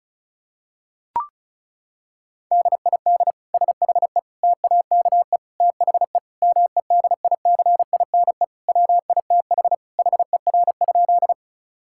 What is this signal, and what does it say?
Morse code, heard as a keyed tone, about 36 wpm, 700 Hz. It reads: DID SHE TAKE THE MEDICINE WITH HER?
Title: Sentences from Top 1000 Words - Review Part 6 - 35wpm